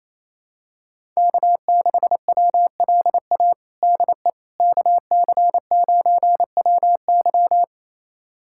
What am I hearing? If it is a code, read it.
Morse code, heard as a keyed tone, about 28 wpm, 700 Hz. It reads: K6WLA DE KC9WY
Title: USA Callsign DE Callsign 28wpm